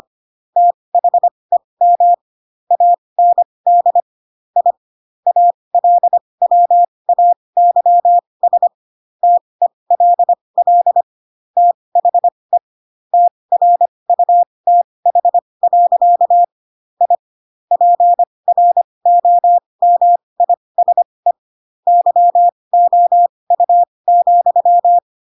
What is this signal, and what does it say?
Morse code, heard as a keyed tone, about 25 wpm, spaced as for 20 wpm, 700 Hz. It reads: THEM AND I ALWAYS TELL THE TRUTH. I PROMISE YOU,
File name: Pinocchio / Pinocchio ch8 15 25 ik1hge